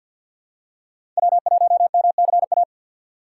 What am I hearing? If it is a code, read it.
Morse code, heard as a keyed tone, about 50 wpm, 700 Hz. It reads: W1MCA
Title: USA Call Signs - 5 Characters - 4x Character Spacing - 50wpm